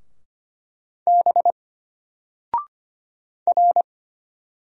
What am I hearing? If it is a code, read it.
Morse code, heard as a keyed tone, about 25 wpm, 700 Hz. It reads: B R
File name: Single-Letter-Number-25wpm